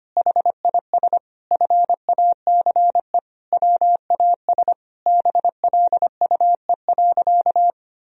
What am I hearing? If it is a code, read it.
Morse code, heard as a keyed tone, about 25 wpm, 700 Hz. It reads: HIS FACE WAS BLUE.
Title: Sentences from Top 1000 Words - Review Part 3 - 25wpm